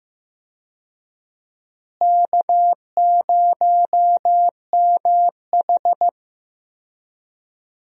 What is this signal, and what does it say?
Morse code, heard as a keyed tone, about 15 wpm, 700 Hz. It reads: K0MH
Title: USA Call Signs - 4 Characters - Repeated 3x - 15wpm